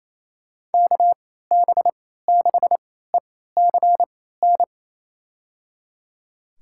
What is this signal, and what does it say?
Morse code, heard as a keyed tone, about 28 wpm, spaced as for 16 wpm, 700 Hz. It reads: KB6ECN
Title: USA Call Signs - 6 Characters - 3x Character Spacing - 28wpm